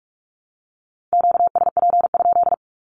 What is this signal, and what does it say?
Morse code, heard as a keyed tone, about 46 wpm, 700 Hz. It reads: QSP?
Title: Q-Codes with Spoken Meaning - 45wpm